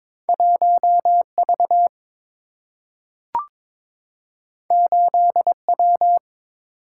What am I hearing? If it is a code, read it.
Morse code, heard as a keyed tone, about 22 wpm, 700 Hz. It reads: 1V 8W
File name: practice / 22wpm / Two Letter-Number Combinations - 22wpm